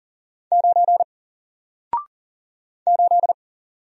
Morse code, heard as a keyed tone, about 40 wpm, 700 Hz. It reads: 9 8
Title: Single Letter-Number 40wpm